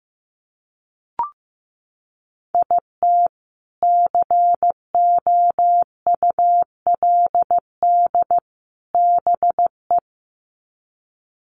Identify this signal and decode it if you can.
Morse code, heard as a keyed tone, about 15 wpm, 700 Hz. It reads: IT COULD BE
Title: Sets of 3 Words 15wpm